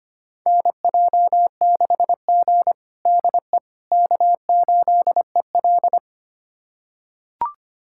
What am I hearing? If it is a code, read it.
Morse code, heard as a keyed tone, about 25 wpm, 700 Hz. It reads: NJ6G DE K8EL
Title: USA Callsign DE Callsign - 4 Characters - 25wpm